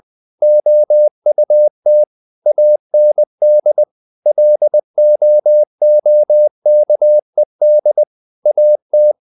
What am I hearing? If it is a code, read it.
Morse code, heard as a keyed tone, about 20 wpm, 600 Hz. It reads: OUT AND LOOKED AT
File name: the old man and the sea part 2 20wpm